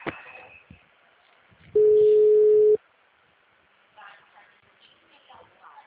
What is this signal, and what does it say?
Ringback tone (Europe, 425 Hz), with speech.